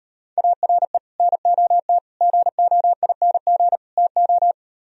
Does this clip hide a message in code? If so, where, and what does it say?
Yes: Morse code from 0.4 to 4.5 s, heard as a keyed tone, about 38 wpm, 700 Hz. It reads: ARE NOT GOING TO